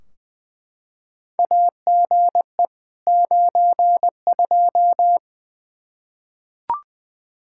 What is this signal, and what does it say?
Morse code, heard as a keyed tone, about 20 wpm, 700 Hz. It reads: AGE 92
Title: Practice QSO - Exchange 3 - More Info - 20wpm